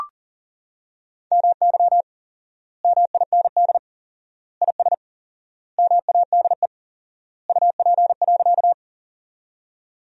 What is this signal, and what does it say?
Morse code, heard as a keyed tone, about 40 wpm, 700 Hz. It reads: MY MIND IS MADE UP.
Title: 4x Word-Spacing - Sentences from Top 500 Words - 40wpm